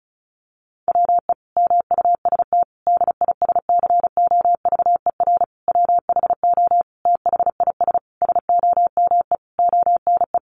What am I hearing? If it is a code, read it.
Morse code, heard as a keyed tone, about 34 wpm, 700 Hz. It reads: WE MUST DISCOVER WHO THIS SOMEONE
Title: Sentences from Top 1100 Words 35wpm